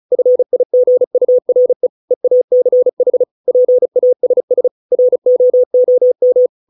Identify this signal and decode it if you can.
Morse code, heard as a keyed tone, about 34 wpm, 500 Hz. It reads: FIGURE EACH PASS ROOM